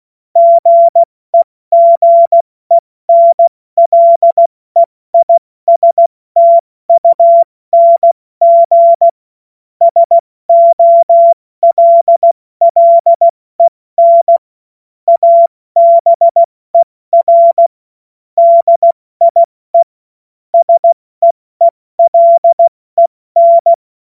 Morse code, heard as a keyed tone, about 16 wpm, 680 Hz. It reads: GEGENLEISTUNG SOLLEN ABER DIE SEELEN
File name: Morsekurs / Witze / Witz 09. August 2022 70 F80